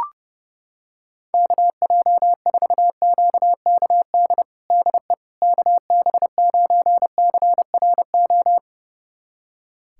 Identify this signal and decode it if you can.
Morse code, heard as a keyed tone, about 32 wpm, 700 Hz. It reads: KJ4QKD DE KB9CRO
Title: USA Callsign DE Callsign - 6 Characters - 30wpm